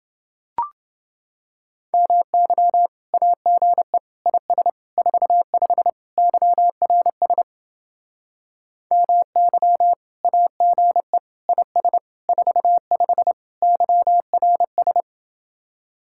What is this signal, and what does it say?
Morse code, heard as a keyed tone, about 29 wpm, 700 Hz. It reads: MY AGE IS 45 YRS MY AGE IS 45 YRS